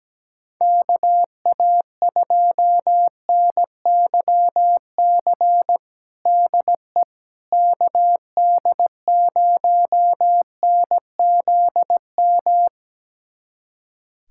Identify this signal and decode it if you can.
Morse code, heard as a keyed tone, about 17 wpm, 700 Hz. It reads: KA2NYC DE KD0NZM
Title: USA Callsign DE Callsign - 6 Characters - 17wpm